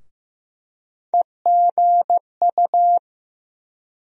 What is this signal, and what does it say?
Morse code, heard as a keyed tone, about 15 wpm, 700 Hz. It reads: EGU
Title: USA Call Sign Postfixes - 3 Letters 15wpm